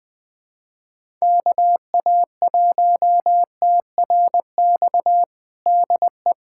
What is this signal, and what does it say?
Morse code, heard as a keyed tone, about 21 wpm, 700 Hz. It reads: KA1TRX DE
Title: USA Callsign DE Callsign - 6 Characters - 20wpm